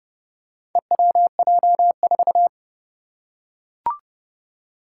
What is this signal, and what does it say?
Morse code, heard as a keyed tone, about 30 wpm, 700 Hz. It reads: EWJ4
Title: Four Challenging Letter-Number Combinations 30wpm